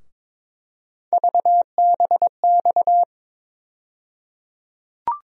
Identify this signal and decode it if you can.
Morse code, heard as a keyed tone, about 22 wpm, 700 Hz. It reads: VBX
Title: USA Call Sign Postfixes - 3 Letters 22wpm